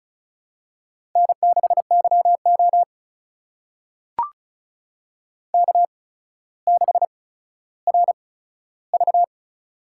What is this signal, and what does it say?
Morse code, heard as a keyed tone, about 36 wpm, spaced as for 15 wpm, 700 Hz. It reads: N6YO K6RV
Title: USA Call Signs - 4 Characters - 8x Character Spacing - 35wpm